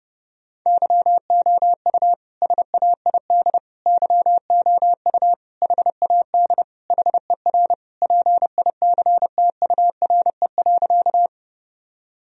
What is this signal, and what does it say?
Morse code, heard as a keyed tone, about 32 wpm, 700 Hz. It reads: YOU SAID YOU HAD HER PICTURE.
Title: Sentences from Top 1000 Words - Review Part 1 - 30wpm